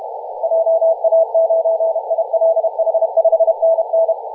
Morse code, heard as a keyed tone, about 31 wpm, 660 Hz. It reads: JA9ILH 5NN
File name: posts / 2016-nov-23-0544z-xx9tkb-24-895mhz-cw